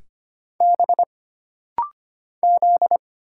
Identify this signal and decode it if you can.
Morse code, heard as a keyed tone, about 25 wpm, 700 Hz. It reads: B Z
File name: Single Character (Letter:Number:Punctuation:Prosign) - Rapid-Fire - 25wpm